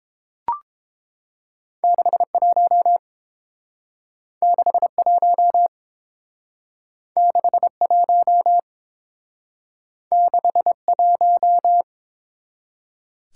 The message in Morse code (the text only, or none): 61 61 61 61